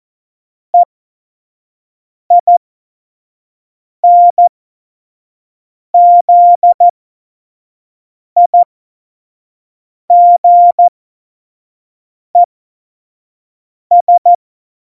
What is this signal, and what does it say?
Morse code, heard as a keyed tone, about 14 wpm, spaced as for 5 wpm, 700 Hz. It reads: EINZIGES
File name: Morsekurs / Witze / Witz 05. Juni 2018, 25 F70